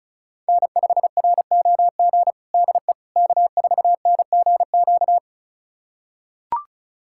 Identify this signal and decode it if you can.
Morse code, heard as a keyed tone, about 36 wpm, 700 Hz. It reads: N5ROG DE K4NGQ
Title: USA Callsign DE Callsign - 5 Characters - 35wpm